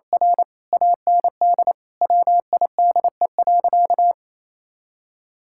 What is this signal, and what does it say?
Morse code, heard as a keyed tone, about 28 wpm, 700 Hz. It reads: R AND WIDE.
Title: Sentences from Top 1000 Words - Review Part 6 - 28wpm